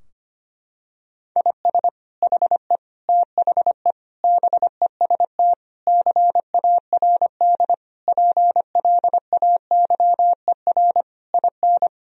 Morse code, heard as a keyed tone, about 25 wpm, 690 Hz. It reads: IS HE THE BEST CARD PLAYER IN
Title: Sentences from Top 1000 Words - Review Part 4 - 25wpm